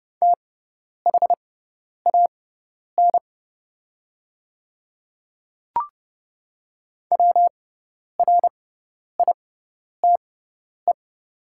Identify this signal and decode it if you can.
Morse code, heard as a keyed tone, about 30 wpm, spaced as for 10 wpm, 700 Hz. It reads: THAN WRITE
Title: 6x Character Spacing - Top 100 Words - 30wpm